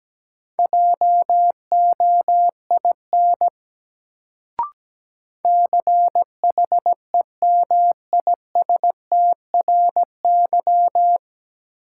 Morse code, heard as a keyed tone, about 17 wpm, 700 Hz. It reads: JOIN CHEMISTRY
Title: Sentences from Top 1000 Words - Review Part 6 - New Word Drill - 17wpm